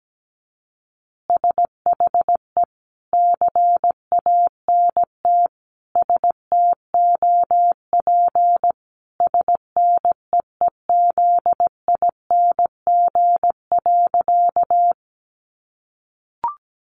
Morse code, heard as a keyed tone, about 17 wpm, 700 Hz. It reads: SHE CANT STOP SNEEZING.
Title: Easy Phrases 17wpm